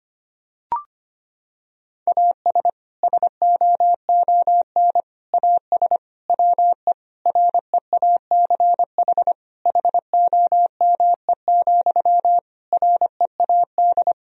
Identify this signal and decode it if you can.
Morse code, heard as a keyed tone, about 25 wpm, 700 Hz. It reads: AS SOON AS WE REACH HOME, READ
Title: Sentences from Top 1000 Words - Review Part 2 - 25wpm